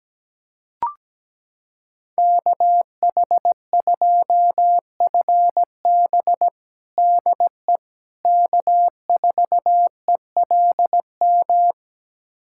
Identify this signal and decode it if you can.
Morse code, heard as a keyed tone, about 17 wpm, 700 Hz. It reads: KH2FB DE K4ELM
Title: USA Callsign DE Callsign - 5 Characters - 17wpm